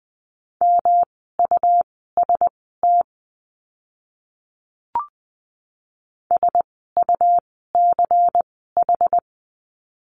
Morse code, heard as a keyed tone, about 20 wpm, spaced as for 15 wpm, 700 Hz. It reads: MUST SUCH
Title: 237 Lesson 38f - Letter q - Words 20wpm